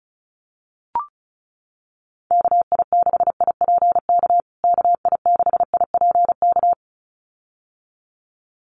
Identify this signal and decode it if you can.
Morse code, heard as a keyed tone, about 34 wpm, 700 Hz. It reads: KI6IPK KI6IPK
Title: USA Call Signs Repeated 2x 35wpm